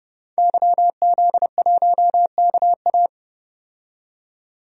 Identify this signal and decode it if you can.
Morse code, heard as a keyed tone, about 30 wpm, 700 Hz. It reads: YZ1KA
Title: DXCC-100 Call Signs - 5 Characters - 30wpm